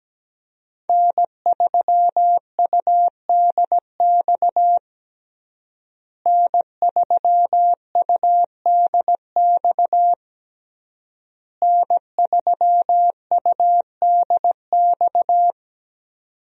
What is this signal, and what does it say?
Morse code, heard as a keyed tone, about 17 wpm, 700 Hz. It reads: N3UDX N3UDX N3UDX
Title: USA Call Signs - 5 Characters - Repeated 3x - 17wpm